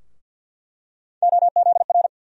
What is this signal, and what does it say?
Morse code, heard as a keyed tone, about 48 wpm, 700 Hz. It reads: OZR
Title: USA Call Sign Postfixes - 3 Letters 50wpm